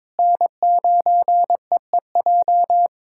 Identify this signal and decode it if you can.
Morse code, heard as a keyed tone, about 22 wpm, 700 Hz. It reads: N9EEJ
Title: USA Callsign DE Callsign 22wpm